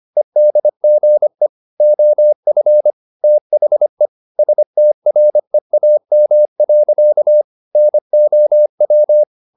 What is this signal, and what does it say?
Morse code, heard as a keyed tone, about 25 wpm, 600 Hz. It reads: EDGE OF THE STREAM. NOW